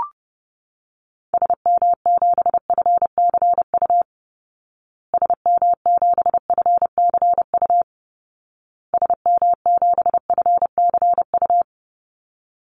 Morse code, heard as a keyed tone, about 30 wpm, 700 Hz. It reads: SM7FCU SM7FCU SM7FCU
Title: Call Signs Repeated 2x - 30wpm